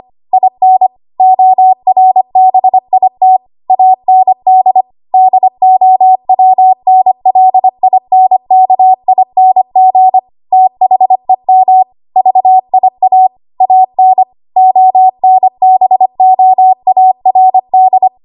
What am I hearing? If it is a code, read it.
Morse code, heard as a keyed tone, about 25 wpm, 750 Hz. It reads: IN ORBIT AND DOWNLINKING THEM VIA AN ONBOARD
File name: Archive / 25 WPM / 25WPM